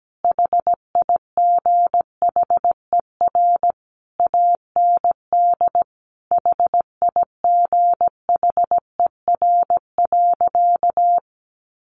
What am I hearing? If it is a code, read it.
Morse code, heard as a keyed tone, about 17 wpm, 700 Hz. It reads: HIGHER AND HIGHER.
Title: Additional Easy Phrases 17wpm